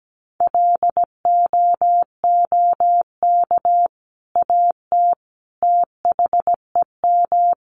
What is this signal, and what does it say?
Morse code, heard as a keyed tone, about 17 wpm, 700 Hz. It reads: LOOK AT THEM
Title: Sentences from Top 1400 Words 17wpm